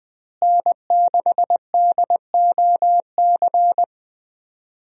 Morse code, heard as a keyed tone, about 20 wpm, 700 Hz. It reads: N6DOC
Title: USA Call Signs - 5 Characters - Repeated 2x - 20wpm